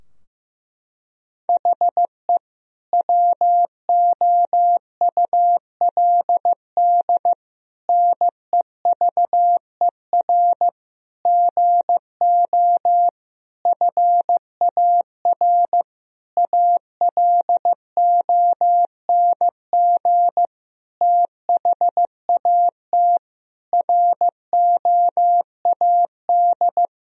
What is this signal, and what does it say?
Morse code, heard as a keyed tone, about 15 wpm, 700 Hz. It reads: HE WOULD NEVER GO FAR ALONG THAT ROAD
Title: Sentences from Top 1000 Words - Review Part 2 - 15wpm